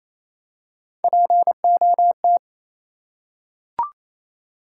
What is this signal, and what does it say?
Morse code, heard as a keyed tone, about 27 wpm, 700 Hz. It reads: POT
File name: Easy Words - Speed-Racing with Hints - 22wpm